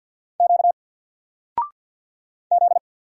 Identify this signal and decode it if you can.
Morse code, heard as a keyed tone, about 50 wpm, 700 Hz. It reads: Q Z
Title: Single Character (Letter:Number:Punctuation:Prosign) 50wpm